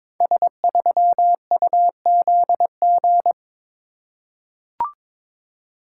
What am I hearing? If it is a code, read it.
Morse code, heard as a keyed tone, about 22 wpm, 690 Hz. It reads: S3UZG
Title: Code Groups - 5 Letter:Numbers - 22wpm